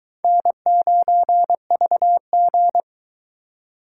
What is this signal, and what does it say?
Morse code, heard as a keyed tone, about 23 wpm, 700 Hz. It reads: N9VG